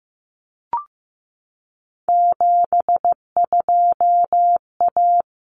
Morse code, heard as a keyed tone, about 15 wpm, 700 Hz. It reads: 72A
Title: Common Cut Numbers - 3-Digit 15wpm